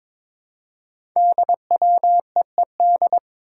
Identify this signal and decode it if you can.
Morse code, heard as a keyed tone, about 22 wpm, 690 Hz. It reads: DWEED
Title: Code Groups - 5 Letter:Numbers - 22wpm